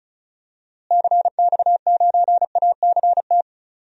Morse code, heard as a keyed tone, about 36 wpm, 700 Hz. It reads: CX9ACT
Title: DXCC-100 Call Signs - 6 Characters - Repeated 2x - 35wpm